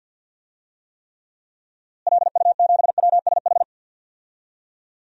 Morse code, heard as a keyed tone, about 49 wpm, 700 Hz. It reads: LU7WSH
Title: DXCC-100 Call Signs - 6 Characters - Repeated 3x - 50wpm